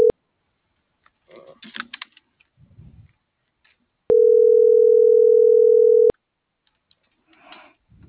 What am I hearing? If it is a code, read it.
Ringback tone (North America), 440 + 480 Hz.